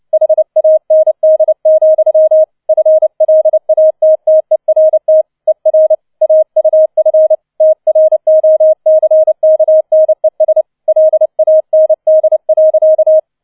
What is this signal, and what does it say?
Morse code, heard as a keyed tone, about 29 wpm, 630 Hz. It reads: HAND, FLATTERT ER AUF TROCKNES LAND.